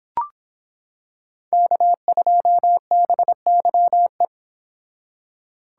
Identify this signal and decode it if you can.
Morse code, heard as a keyed tone, about 25 wpm, 700 Hz. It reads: K2BYE